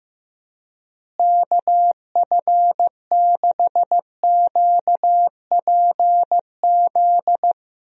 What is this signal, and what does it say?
Morse code, heard as a keyed tone, about 16 wpm, 700 Hz. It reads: KF6QPZ
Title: USA Call Signs - 6 Characters - Repeated 3x - 15wpm